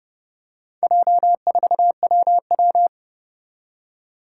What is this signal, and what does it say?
Morse code, heard as a keyed tone, about 30 wpm, 700 Hz. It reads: J4WW
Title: Four Challenging Letter-Number Combinations 30wpm